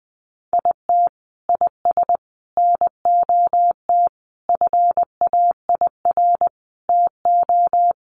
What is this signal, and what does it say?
Morse code, heard as a keyed tone, about 20 wpm, 700 Hz. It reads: IT IS NOT FAIR TO
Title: Sentences from Top 1300 Words 20wpm